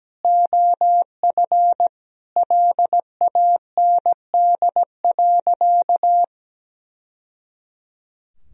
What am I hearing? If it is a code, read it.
Morse code, heard as a keyed tone, about 17 wpm, 700 Hz. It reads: OF LAND.
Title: Sentences from Top 1000 Words - Review Part 2 - 17wpm